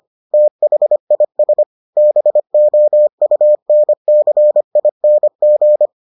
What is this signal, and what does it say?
Morse code, heard as a keyed tone, about 25 wpm, 600 Hz. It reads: THIS BOUNCING